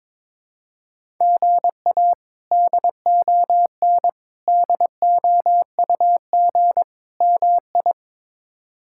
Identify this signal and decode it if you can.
Morse code, heard as a keyed tone, about 22 wpm, 700 Hz. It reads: GA DON DOUG MI
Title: K1USN SST QSO Practice - 22wpm